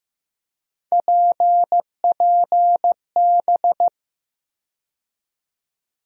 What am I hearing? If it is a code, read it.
Morse code, heard as a keyed tone, about 15 wpm, 700 Hz. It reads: PPB
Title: USA Call Sign Postfixes - 3 Letters 15wpm